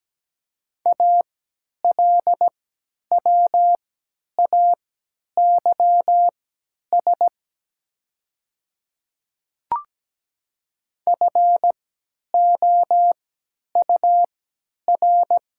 Morse code, heard as a keyed tone, about 17 wpm, spaced as for 10 wpm, 700 Hz. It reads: ALWAYS FOUR
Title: 3x Character Spacing - Top 300 Words - 17wpm